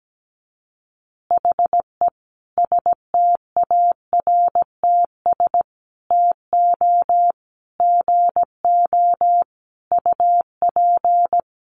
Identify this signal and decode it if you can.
Morse code, heard as a keyed tone, about 17 wpm, 700 Hz. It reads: HE STARTS TO GO UP
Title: Additional Easy Phrases 17wpm